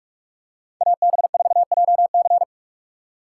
Morse code, heard as a keyed tone, about 42 wpm, 700 Hz. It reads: AB4JC
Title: USA Call Signs - 2x Character Spacing - 45wpm